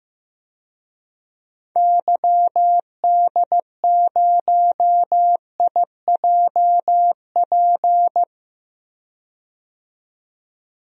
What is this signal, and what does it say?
Morse code, heard as a keyed tone, about 15 wpm, 700 Hz. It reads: YD0IJP
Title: DXCC-100 Call Signs - 6 Characters - 15wpm